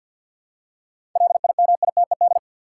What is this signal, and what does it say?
Morse code, heard as a keyed tone, about 51 wpm, 700 Hz. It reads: LIMITED
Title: Seven Letter Words - 50wpm